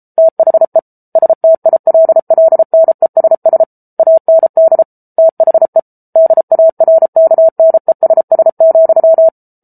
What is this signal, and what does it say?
Morse code, heard as a keyed tone, about 34 wpm, 650 Hz. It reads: THE STILLNESS AND THE DARKNESS,